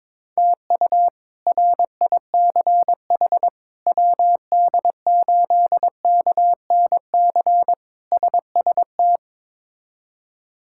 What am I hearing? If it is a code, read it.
Morse code, heard as a keyed tone, about 22 wpm, 700 Hz. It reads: TU RICH WD8KNC SST
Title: K1USN SST QSO Practice - 22wpm